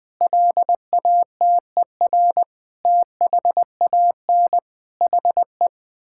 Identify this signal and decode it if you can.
Morse code, heard as a keyed tone, about 20 wpm, 700 Hz. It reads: LATER THAN HE
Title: Sentences from Top 1600 Words 20wpm